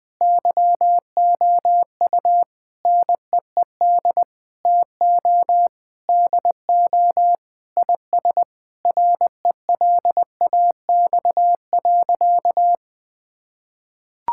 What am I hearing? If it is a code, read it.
Morse code, heard as a keyed tone, about 20 wpm, 700 Hz. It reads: YOU NEED TO DO IS RELAX.
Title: Sentences with 8-Word Phrases - Spoken Beginning - 20wpm